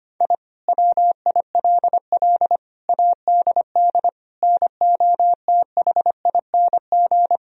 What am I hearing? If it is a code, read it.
Morse code, heard as a keyed tone, about 25 wpm, 700 Hz. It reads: I WILL ADD NOTHING
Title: Sentences from Top 1000 Words - Review Part 1 - 25wpm